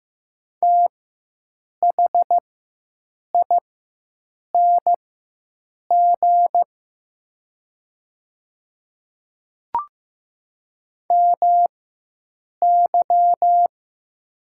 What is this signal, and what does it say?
Morse code, heard as a keyed tone, about 15 wpm, spaced as for 7 wpm, 700 Hz. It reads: THING MY
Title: 4x Character Spacing - Top 100 Words - 15wpm